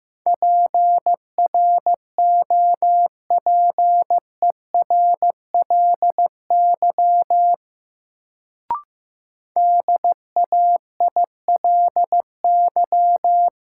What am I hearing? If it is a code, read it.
Morse code, heard as a keyed tone, about 15 wpm, 700 Hz. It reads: PROPERLY DAILY
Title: Sentences from Top 1100 Words - New Word Drill - 15wpm